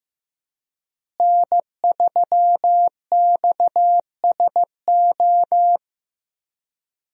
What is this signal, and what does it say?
Morse code, heard as a keyed tone, about 15 wpm, 700 Hz. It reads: N3XSO
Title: USA Call Signs - 5 Characters - Repeated 3x - 15wpm